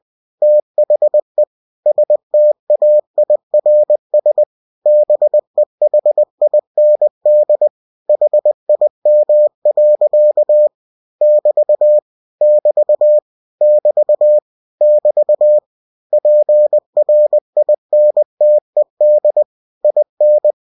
Morse code, heard as a keyed tone, about 20 wpm, 600 Hz. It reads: THE STAIRS BEHIND HIM. <BT> <BT> <BT> <BT> PRINTED IN
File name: cw-books / winnie-the-pooh / winnie-20wpm-page-0111